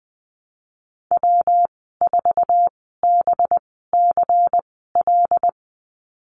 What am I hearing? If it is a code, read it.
Morse code, heard as a keyed tone, about 20 wpm, spaced as for 15 wpm, 700 Hz. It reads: W4BCL
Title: 186 Lesson 30g - Letter b - Call Signs 20wpm